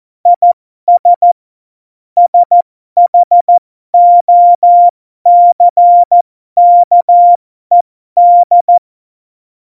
Morse code, heard as a keyed tone, about 14 wpm, spaced as for 12 wpm, 700 Hz. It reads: IS SHOCKED
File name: Morsekurs / Witze